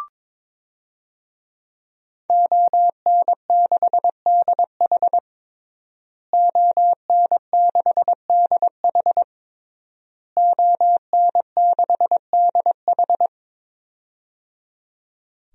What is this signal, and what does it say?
Morse code, heard as a keyed tone, about 22 wpm, 700 Hz. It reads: ON6DH ON6DH ON6DH
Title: DXCC-100 Call Signs - 5 Characters - Repeated 3x - 22wpm